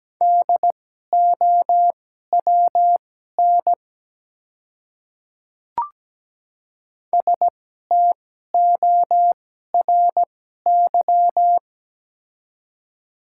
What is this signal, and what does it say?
Morse code, heard as a keyed tone, about 17 wpm, spaced as for 12 wpm, 690 Hz. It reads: DOWN STORY
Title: 2x Character Spacing - Top 300 Words - 17wpm